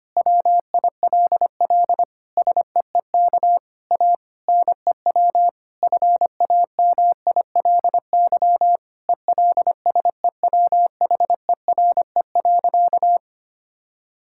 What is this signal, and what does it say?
Morse code, heard as a keyed tone, about 25 wpm, 700 Hz. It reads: WILL SEEK A NEW FAMILY ELSEWHERE.
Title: Sentences from Top 1600 Words 25wpm